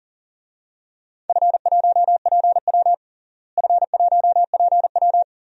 Morse code, heard as a keyed tone, about 37 wpm, 700 Hz. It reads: F1PW F1PW
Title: DXCC-100 Call Signs - 4 Characters - Repeated 2x - 40wpm